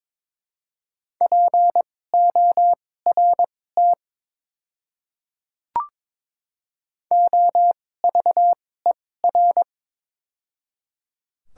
Morse code, heard as a keyed tone, about 22 wpm, spaced as for 16 wpm, 700 Hz. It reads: PORT OVER
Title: 2x Character Spacing - Top 300 Words - 22wpm